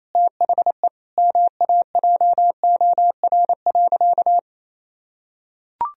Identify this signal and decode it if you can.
Morse code, heard as a keyed tone, about 28 wpm, 700 Hz. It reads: THE MAJOR.
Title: Sentences from Top 1000 Words - Review Part 4 - 28wpm